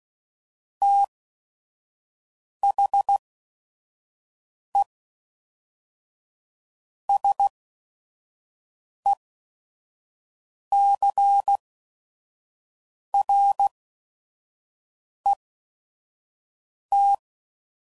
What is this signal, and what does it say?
Morse code, heard as a keyed tone, about 16 wpm, spaced as for 5 wpm, 800 Hz. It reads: THE SECRET
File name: CW Code Groups @5WPM / Technical Subjects / SwPwrSup